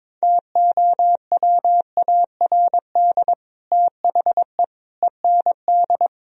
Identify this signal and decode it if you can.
Morse code, heard as a keyed tone, about 22 wpm, 700 Hz. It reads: TOWARD THE END
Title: Speed-Racing - Sets of 5 Words 22wpm